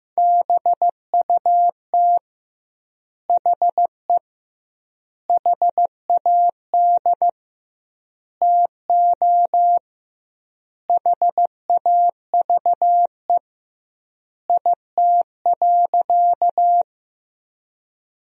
Morse code, heard as a keyed tone, about 15 wpm, 700 Hz. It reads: BUT HE HAD TO HAVE IT.
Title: 2x Word-Spacing - Sentences from Top 100 Words - 15wpm